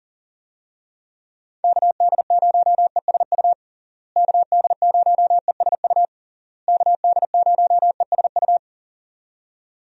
Morse code, heard as a keyed tone, about 40 wpm, 700 Hz. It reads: KD0ESU KD0ESU KD0ESU
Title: USA Call Signs - 6 Characters - Repeated 3x - 40wpm